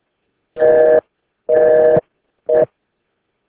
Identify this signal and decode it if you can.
Busy tone (North America), 480 + 620 Hz, with speech.